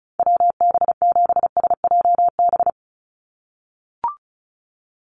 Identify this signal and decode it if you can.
Morse code, heard as a keyed tone, about 34 wpm, 700 Hz. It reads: WB7SJB
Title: USA Call Signs 35wpm